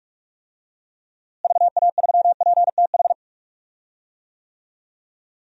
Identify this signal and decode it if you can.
Morse code, heard as a keyed tone, about 42 wpm, 700 Hz. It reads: VA3PTH
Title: DXCC-100 Call Signs - 6 Characters - Repeated 2x - 45wpm